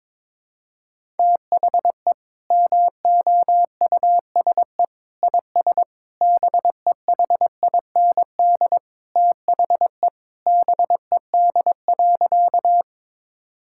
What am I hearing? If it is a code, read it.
Morse code, heard as a keyed tone, about 22 wpm, 700 Hz. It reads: THE MOUSE IS BEHIND THE BED.
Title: Sentences from Top 1000 Words - Review Part 5 - 22wpm